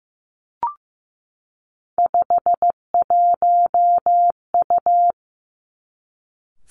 Morse code, heard as a keyed tone, about 15 wpm, 700 Hz. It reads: 51U
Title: Three Challenging Letter-Number Combinations 15wpm